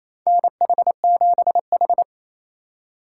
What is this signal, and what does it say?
Morse code, heard as a keyed tone, about 28 wpm, 700 Hz. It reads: NH7H
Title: USA Call Signs - 4 Characters - 3x Character Spacing - 28wpm